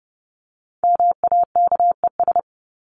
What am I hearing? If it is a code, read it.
Morse code, heard as a keyed tone, about 30 wpm, 700 Hz. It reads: MAKES
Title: ES:S Suffix - Top 100 Words with Base Words 30wpm